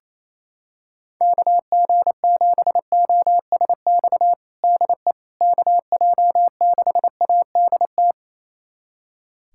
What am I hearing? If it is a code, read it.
Morse code, heard as a keyed tone, about 27 wpm, 700 Hz. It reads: KG7OSX DE KJ6ADT
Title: USA Callsign DE Callsign 28wpm